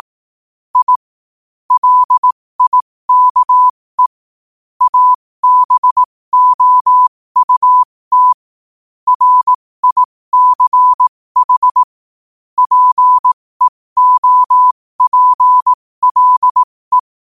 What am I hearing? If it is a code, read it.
Morse code, heard as a keyed tone, about 18 wpm, spaced as for 15 wpm, 1000 Hz. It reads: I LIKE ABOUT RICH PEOPLE
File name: qotdfiles / 15wpm